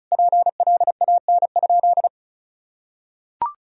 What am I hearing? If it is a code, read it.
Morse code, heard as a keyed tone, about 34 wpm, 700 Hz. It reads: PLAN?
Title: Sentences from Top 1000 Words - Review Part 2 - 35wpm